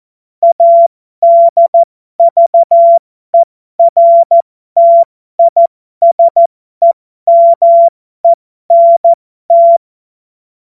Morse code, heard as a keyed tone, about 14 wpm, spaced as for 12 wpm, 680 Hz. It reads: ADVERTISEMENT
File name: Morsekurs / Witze